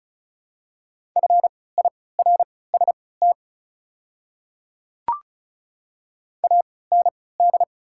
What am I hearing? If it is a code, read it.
Morse code, heard as a keyed tone, about 34 wpm, spaced as for 20 wpm, 690 Hz. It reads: FIRST AND
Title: 3x Character Spacing - Top 100 Words - 35wpm